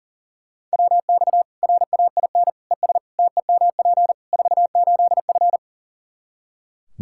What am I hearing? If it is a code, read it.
Morse code, heard as a keyed tone, about 40 wpm, 700 Hz. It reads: WX RAIN ES TEMP 48F
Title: Practice QSO - Exchange 2 - Info - 40wpm